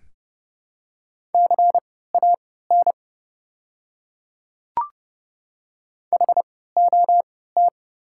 Morse code, heard as a keyed tone, about 32 wpm, spaced as for 17 wpm, 700 Hz. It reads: CAN HOT
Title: 3x Character Spacing - Top 200 Words - 30wpm